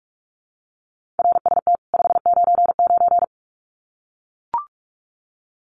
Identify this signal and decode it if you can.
Morse code, heard as a keyed tone, about 44 wpm, 700 Hz. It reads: RST 599
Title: Common QSO Phrases 45wpm